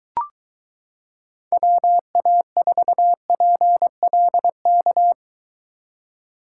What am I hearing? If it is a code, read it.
Morse code, heard as a keyed tone, about 23 wpm, 700 Hz. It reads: WA4PLK